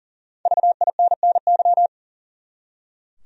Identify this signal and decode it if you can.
Morse code, heard as a keyed tone, about 38 wpm, 700 Hz. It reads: VINNY
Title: USA Common Names - Max 5 Letters - 40wpm